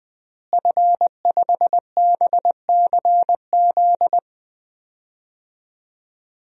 Morse code, heard as a keyed tone, about 21 wpm, 700 Hz. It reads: F5BCZ
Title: DXCC-100 Call Signs - 5 Characters - Repeated 3x - 20wpm